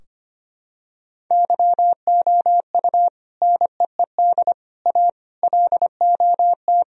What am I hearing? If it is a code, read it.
Morse code, heard as a keyed tone, about 25 wpm, 700 Hz. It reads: YOU NEED A LOT
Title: Sentences from Top 1000 Words - Review Part 3 - 25wpm